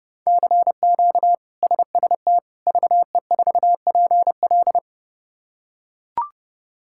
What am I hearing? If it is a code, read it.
Morse code, heard as a keyed tone, about 30 wpm, 700 Hz. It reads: CQ SST VE4PL
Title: K1USN SST QSO Practice - 30wpm